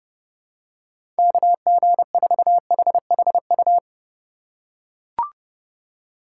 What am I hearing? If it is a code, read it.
Morse code, heard as a keyed tone, about 31 wpm, 700 Hz. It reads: KG4HHU